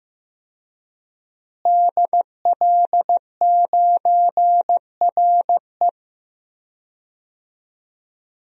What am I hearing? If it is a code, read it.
Morse code, heard as a keyed tone, about 15 wpm, 700 Hz. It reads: DL9RE
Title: DXCC-100 Call Signs - 5 Characters - Repeated 2x - 15wpm